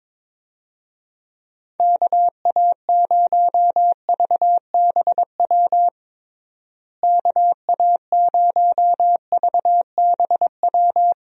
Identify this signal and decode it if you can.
Morse code, heard as a keyed tone, about 22 wpm, 700 Hz. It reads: KA0VBW KA0VBW
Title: USA Call Signs - 6 Characters - Repeated 2x - 22wpm